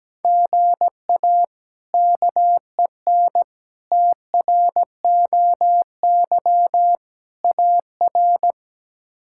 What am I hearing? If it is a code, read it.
Morse code, heard as a keyed tone, about 17 wpm, 700 Hz. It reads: GA KEN TROY AR
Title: K1USN SST QSO Practice - 17wpm